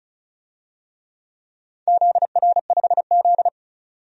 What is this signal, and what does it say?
Morse code, heard as a keyed tone, about 34 wpm, 700 Hz. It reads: ZR5Z
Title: DXCC-100 Call Signs - 4 Characters - 35wpm